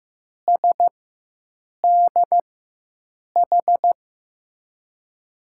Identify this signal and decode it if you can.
Morse code, heard as a keyed tone, about 15 wpm, spaced as for 7 wpm, 700 Hz. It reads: SDH
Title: USA Call Signs - 4x Character Spacing - 15wpm